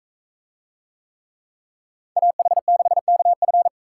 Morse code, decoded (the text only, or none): AH6KF